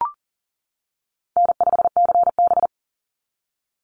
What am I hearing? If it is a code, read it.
Morse code, heard as a keyed tone, about 40 wpm, 700 Hz. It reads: N5CB
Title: USA Call Signs 40wpm